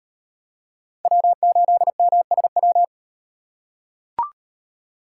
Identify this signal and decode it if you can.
Morse code, heard as a keyed tone, about 38 wpm, 700 Hz. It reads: W8MSW